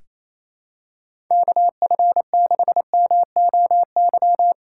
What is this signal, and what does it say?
Morse code, heard as a keyed tone, about 28 wpm, 700 Hz. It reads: KF6MOY
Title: USA Call Signs - 6 Characters - 2x Character Spacing - 28wpm